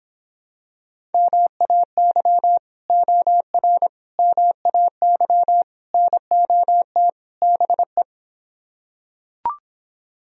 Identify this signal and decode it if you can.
Morse code, heard as a keyed tone, about 26 wpm, 700 Hz. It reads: MAY OR MAY NOT BE